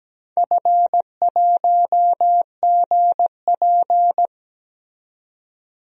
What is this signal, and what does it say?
Morse code, heard as a keyed tone, about 17 wpm, 700 Hz. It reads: F1GP
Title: DXCC-100 Call Signs - 4 Characters - 17wpm